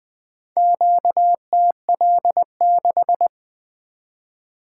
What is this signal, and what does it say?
Morse code, heard as a keyed tone, about 20 wpm, 700 Hz. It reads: QTL6
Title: Four Letter-Number Combinations - 20wpm